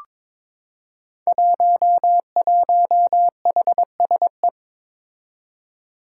Morse code, heard as a keyed tone, about 22 wpm, 700 Hz. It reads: 11HSE
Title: Five Challenging Letter-Number Combinations 22wpm